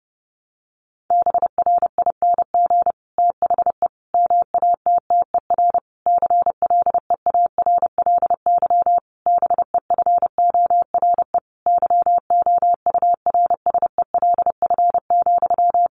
Morse code, heard as a keyed tone, about 30 wpm, 700 Hz. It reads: BRING THE MATTER CLEARLY BEFORE YOURSELF,
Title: Sentences from Top 1300 Words 30wpm